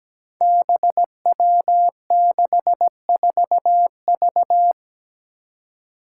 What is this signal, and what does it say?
Morse code, heard as a keyed tone, about 17 wpm, 700 Hz. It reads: BW64V
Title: Five Challenging Letter-Number Combinations 17wpm